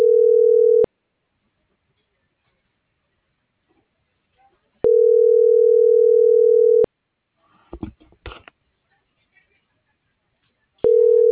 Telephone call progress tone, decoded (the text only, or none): ringback tone